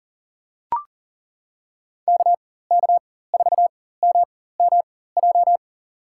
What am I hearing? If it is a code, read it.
Morse code, heard as a keyed tone, about 40 wpm, spaced as for 19 wpm, 690 Hz. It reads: KK4MMJ
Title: USA Call Signs - 4x Character Spacing - 40wpm